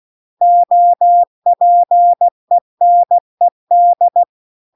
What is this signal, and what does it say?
Morse code, heard as a keyed tone, about 16 wpm, 700 Hz. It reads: OPENED
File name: CW-The-Big-Twoheaded-River-2-16WPM